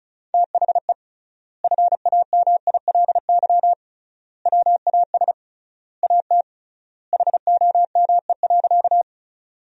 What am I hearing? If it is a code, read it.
Morse code, heard as a keyed tone, about 36 wpm, 700 Hz. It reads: THE FAMILY WAS AT HOME.
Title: 3x Word-Spacing - Sentences from Top 400 Words - 35wpm